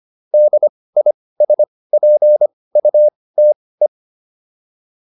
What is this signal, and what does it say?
Morse code, heard as a keyed tone, about 25 wpm, spaced as for 18 wpm, 600 Hz. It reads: DISPUTE